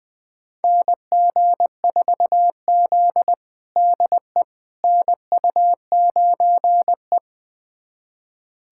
Morse code, heard as a keyed tone, about 20 wpm, 700 Hz. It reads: NG4Z DE NU9E
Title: USA Callsign DE Callsign - 4 Characters - 20wpm